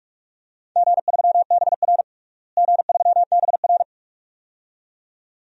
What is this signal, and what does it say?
Morse code, heard as a keyed tone, about 46 wpm, 700 Hz. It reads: G3BR G3BR
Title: DXCC-100 Call Signs - 4 Characters - Repeated 2x - 45wpm